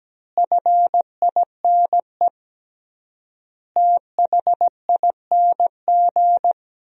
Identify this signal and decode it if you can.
Morse code, heard as a keyed tone, about 17 wpm, 700 Hz. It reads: FINE THING
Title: 3x Word-Spacing - Sentences from Top 500 Words - 17wpm